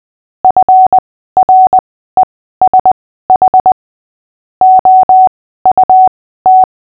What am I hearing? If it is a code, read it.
Morse code, heard as a keyed tone, about 20 wpm, spaced as for 14 wpm, 750 Hz. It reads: FRESH OUT
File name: cw / fists / Short-Story-8-14wpm